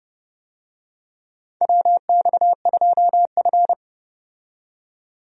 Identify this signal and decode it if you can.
Morse code, heard as a keyed tone, about 30 wpm, 700 Hz. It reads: WX2F
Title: USA Call Signs - 4 Characters - 30wpm